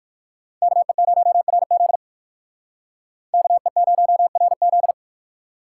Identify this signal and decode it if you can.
Morse code, heard as a keyed tone, about 59 wpm, 700 Hz. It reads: KE0RZ KE0RZ